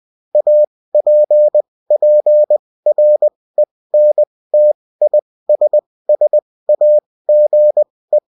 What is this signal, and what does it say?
Morse code, heard as a keyed tone, about 20 wpm, spaced as for 16 wpm, 600 Hz. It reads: APPRENTISSAGE